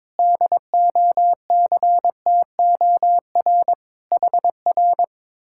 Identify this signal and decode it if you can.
Morse code, heard as a keyed tone, about 22 wpm, 700 Hz. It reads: DOCTOR HR
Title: Speed-Racing Common QSO Phrases - 22wpm